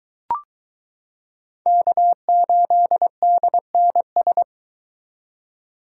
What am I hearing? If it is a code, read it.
Morse code, heard as a keyed tone, about 22 wpm, 700 Hz. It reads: K8DNS